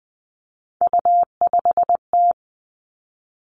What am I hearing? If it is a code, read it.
Morse code, heard as a keyed tone, about 20 wpm, 700 Hz. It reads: U5T
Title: Three Challenging Letter-Number Combinations 20wpm